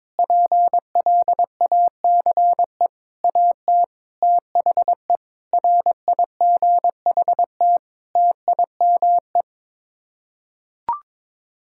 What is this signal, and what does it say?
Morse code, heard as a keyed tone, about 22 wpm, 700 Hz. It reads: PLACE AT THE RIGHT TIME
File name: Sets of 9 Words - 22wpm